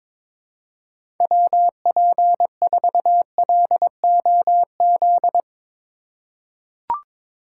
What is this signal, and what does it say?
Morse code, heard as a keyed tone, about 22 wpm, 700 Hz. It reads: WP4LOZ
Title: USA Call Signs - 6 Characters - 2x Character Spacing - 22wpm